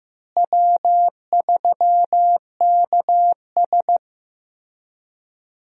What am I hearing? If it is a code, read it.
Morse code, heard as a keyed tone, about 15 wpm, 700 Hz. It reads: W3KS
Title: USA Call Signs - 4 Characters - Repeated 3x - 15wpm